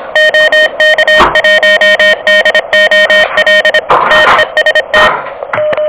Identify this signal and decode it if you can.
Morse code, heard as a keyed tone, about 25 wpm, 1850 Hz, with noise. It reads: OK1DOL TEST
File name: OK QRP 2023 / OK QRP 2023